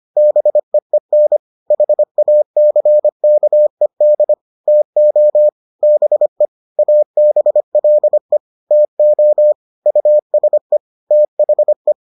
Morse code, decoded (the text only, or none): BEEN HACKED TO BE ABLE TO USE THE